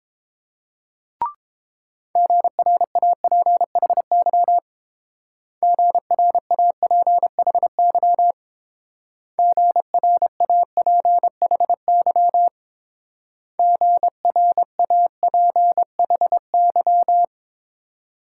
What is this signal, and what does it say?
Morse code, heard as a keyed tone, about 28 wpm, 700 Hz. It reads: GRAPHY GRAPHY GRAPHY GRAPHY